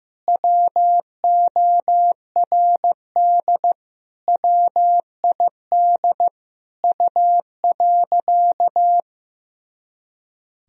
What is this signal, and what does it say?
Morse code, heard as a keyed tone, about 16 wpm, 700 Hz. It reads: WORD WID U.
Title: Sentences from Top 300 Words and Abbreviations 15wpm